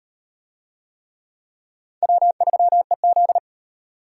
Morse code, heard as a keyed tone, about 36 wpm, 700 Hz. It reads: W3EZ